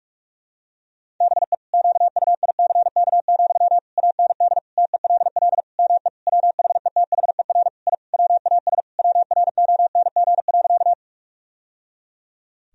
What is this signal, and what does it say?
Morse code, heard as a keyed tone, about 46 wpm, 700 Hz. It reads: BE QUICK, AND TELL ME WHETHER I WAS WRONG.
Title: Sentences from Top 1000 Words - Review Part 3 - 45wpm